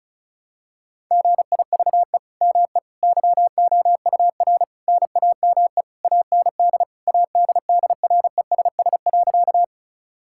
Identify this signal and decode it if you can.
Morse code, heard as a keyed tone, about 36 wpm, 700 Hz. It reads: GIVE ME YOUR NAME AND ADDRESS.
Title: Sentences from Top 1000 Words - Review Part 6 - 35wpm